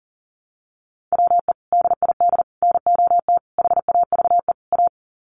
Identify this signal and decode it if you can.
Morse code, heard as a keyed tone, about 37 wpm, 700 Hz. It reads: WE DID NOT HAVE A
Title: Sets of 5 Words Encore2 40wpm